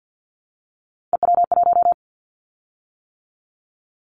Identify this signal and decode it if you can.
Morse code, heard as a keyed tone, about 50 wpm, 700 Hz. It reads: EW1
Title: Callsign Prefixes for Top 50 DXCC - 50wpm